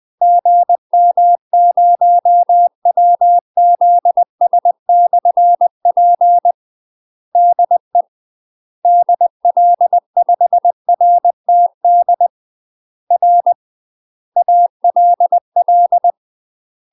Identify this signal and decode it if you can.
Morse code, heard as a keyed tone, about 20 wpm, 700 Hz. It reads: GM0WZS/P DE DL5RTD R ALL
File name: UK MASTER QSOs TRAINING / 20WPM / UK MASTER QSOs 0058